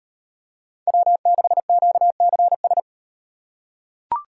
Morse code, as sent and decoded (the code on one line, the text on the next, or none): .-- -.... --.- -.-. ...
W6QCS